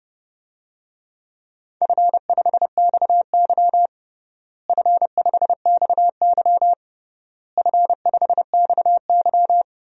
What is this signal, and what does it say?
Morse code, heard as a keyed tone, about 30 wpm, 700 Hz. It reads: F5XY F5XY F5XY
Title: DXCC-100 Call Signs - 4 Characters - Repeated 3x - 30wpm